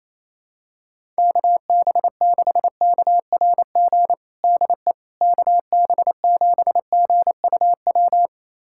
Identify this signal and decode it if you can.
Morse code, heard as a keyed tone, about 28 wpm, 700 Hz. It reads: KB6KRG DE KB7GUW
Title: USA Callsign DE Callsign - 6 Characters - 28wpm